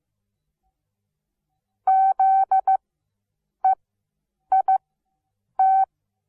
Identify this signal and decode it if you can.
Morse code, heard as a keyed tone, about 15 wpm, spaced as for 8 wpm, 770 Hz. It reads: ZEIT
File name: Onlinelehrgang / morsen / DJ4UF Morsen17